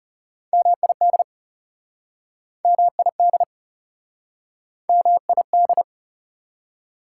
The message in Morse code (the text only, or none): MID MID MID